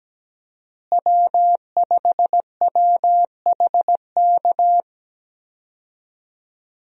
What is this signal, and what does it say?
Morse code, heard as a keyed tone, about 17 wpm, 700 Hz. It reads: W5WHK
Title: USA Call Signs - 5 Characters - Repeated 3x - 17wpm